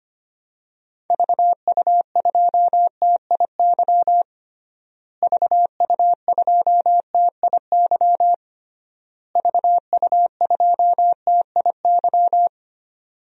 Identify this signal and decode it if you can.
Morse code, heard as a keyed tone, about 25 wpm, 700 Hz. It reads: VU2TIY VU2TIY VU2TIY
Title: DXCC-100 Call Signs - 6 Characters - Repeated 3x - 25wpm